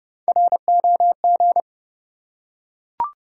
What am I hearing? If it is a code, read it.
Morse code, heard as a keyed tone, about 30 wpm, 700 Hz. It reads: ROG
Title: USA Common Names - Max 3 Letters - 30wpm